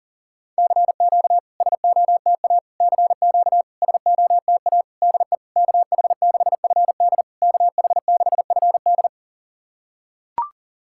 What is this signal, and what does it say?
Morse code, heard as a keyed tone, about 40 wpm, 690 Hz. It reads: CQ SOTA CQ SOTA DE KH6FD KH6FD
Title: SOTA QSO Practice 40wpm